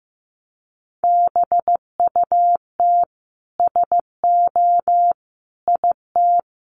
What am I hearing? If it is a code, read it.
Morse code, heard as a keyed tone, about 15 wpm, 700 Hz. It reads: BUT SO IT
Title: Sentences from Top 100 Words 15wpm